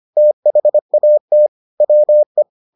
Morse code, heard as a keyed tone, about 25 wpm, 600 Hz. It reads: THAT WE